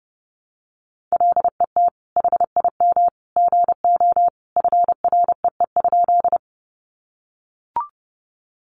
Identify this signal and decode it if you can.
Morse code, heard as a keyed tone, about 30 wpm, 700 Hz. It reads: LET HIM GO FREE?
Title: Sentences from Top 500 Words 30wpm